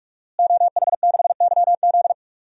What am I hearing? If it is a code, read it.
Morse code, heard as a keyed tone, about 46 wpm, 700 Hz. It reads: OH6YZ
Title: DXCC-100 Call Signs - 5 Characters - Repeated 2x - 45wpm